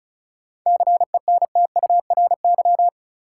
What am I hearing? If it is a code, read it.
Morse code, heard as a keyed tone, about 34 wpm, 700 Hz. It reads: CENTURY
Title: Sets of 5 Words - New Word Drill - 35wpm